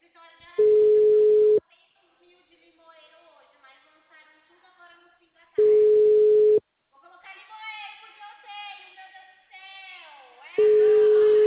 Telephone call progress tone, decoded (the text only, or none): ringback tone